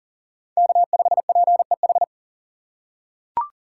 Morse code, heard as a keyed tone, about 40 wpm, 700 Hz. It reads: K5PEH
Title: USA Call Signs - 4x Character Spacing - 40wpm